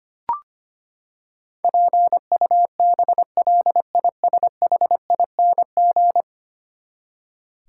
Morse code, heard as a keyed tone, about 25 wpm, 700 Hz. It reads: PUBLISHING
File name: Ten Letter Words - 25wpm